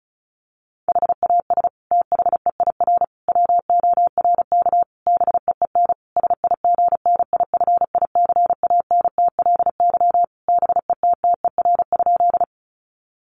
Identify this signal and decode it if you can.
Morse code, heard as a keyed tone, about 34 wpm, 700 Hz. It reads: HAS THEIR WORK BEEN SIGNIFICANTLY BETTER?
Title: Sentences from Top 1500 Words 35wpm